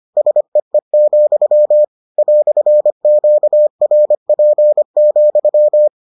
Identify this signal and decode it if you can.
Morse code, heard as a keyed tone, about 25 wpm, 600 Hz. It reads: SEE, "QRP,